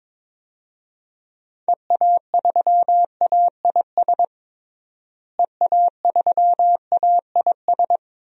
Morse code, heard as a keyed tone, about 22 wpm, 700 Hz. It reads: EA3AIS EA3AIS
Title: DXCC-100 Call Signs - 6 Characters - Repeated 2x - 22wpm